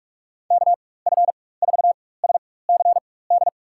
Morse code, heard as a keyed tone, about 46 wpm, spaced as for 21 wpm, 700 Hz. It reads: KF4SCD
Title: USA Call Signs - 4x Character Spacing - 45wpm